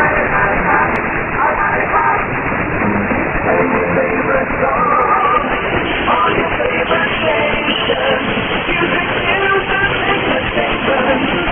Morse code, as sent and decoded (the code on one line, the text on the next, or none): none